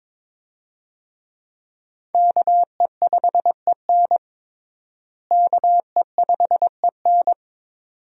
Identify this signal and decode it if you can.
Morse code, heard as a keyed tone, about 22 wpm, 700 Hz. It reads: KE5EN KE5EN
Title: USA Call Signs - 5 Characters - Repeated 2x - 22wpm